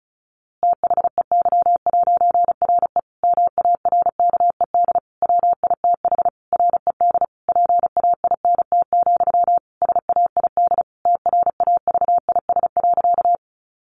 Morse code, heard as a keyed tone, about 34 wpm, 700 Hz. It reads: THEY'RE MARKED WITH RED PAINT, SAID TRAVIS.
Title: A Sound Of Thunder 35wpm